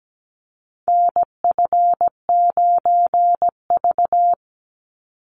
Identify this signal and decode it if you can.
Morse code, heard as a keyed tone, about 17 wpm, 700 Hz. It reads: NF9V
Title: Call Signs - 17wpm